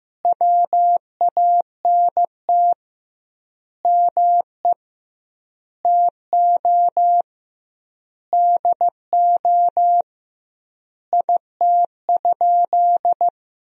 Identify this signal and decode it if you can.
Morse code, heard as a keyed tone, about 15 wpm, 700 Hz. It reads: WANT ME TO DO IT?
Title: 2x Word-Spacing - Sentences from Top 200 Words - 15wpm